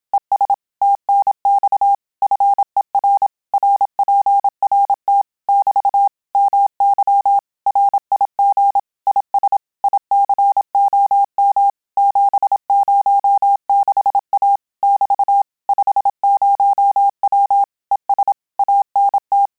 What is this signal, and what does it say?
Morse code, heard as a keyed tone, about 27 wpm, 800 Hz. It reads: ES TNX FER RPRT <BT> MY RIG IS ICOM 706A <BT> 50W ES ANT
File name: Telegraphy-group / QSO CW 25 PPM